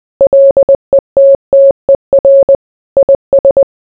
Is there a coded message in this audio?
Yes: Morse code from 0.2 to 3.6 s, heard as a keyed tone, about 20 wpm, 550 Hz. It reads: LETTER IS